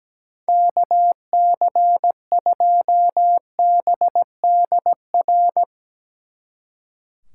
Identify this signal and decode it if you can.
Morse code, heard as a keyed tone, about 19 wpm, 700 Hz. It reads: KC2BDR
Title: Speed Racing - USA 2x3 Call Signs 17wpm